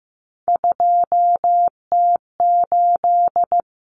Morse code, heard as a keyed tone, about 15 wpm, 700 Hz. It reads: 2T8
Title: Common Cut Numbers - 3-Digit 15wpm